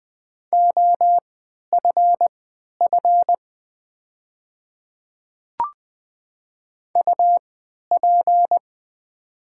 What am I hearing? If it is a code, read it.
Morse code, heard as a keyed tone, about 20 wpm, spaced as for 11 wpm, 690 Hz. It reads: OFF UP
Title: 3x Character Spacing - Top 200 Words - 20wpm